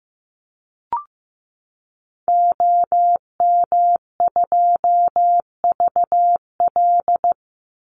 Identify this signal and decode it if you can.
Morse code, heard as a keyed tone, about 15 wpm, 700 Hz. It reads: OM2VL
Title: Call Signs Repeated 2x - 15wpm